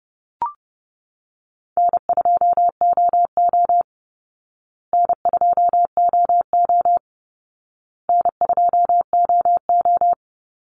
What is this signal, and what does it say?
Morse code, heard as a keyed tone, about 30 wpm, 700 Hz. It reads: N2OO N2OO N2OO
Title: Easy Call Signs Repeated 3x - 30wpm